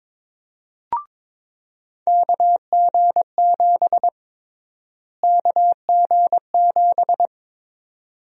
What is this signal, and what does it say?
Morse code, heard as a keyed tone, about 21 wpm, 700 Hz. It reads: KG7 KG7
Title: USA Call Sign Prefixes - Repeated 2x - 22wpm